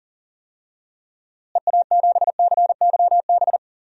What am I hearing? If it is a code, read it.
Morse code, heard as a keyed tone, about 40 wpm, 700 Hz. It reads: EA7CYB
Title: DXCC-100 Call Signs - 6 Characters - Repeated 3x - 40wpm